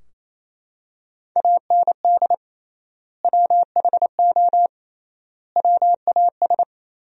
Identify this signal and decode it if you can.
Morse code, heard as a keyed tone, about 27 wpm, 700 Hz. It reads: AND WHO WAS
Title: 3x Word-Spacing - Sentences from Top 100 Words - 28wpm